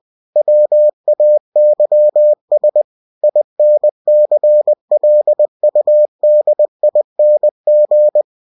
Morse code, heard as a keyed tone, about 20 wpm, 600 Hz. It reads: WAYS INCLUDING
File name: cw-books / winnie-the-pooh / winnie-20wpm-page-0122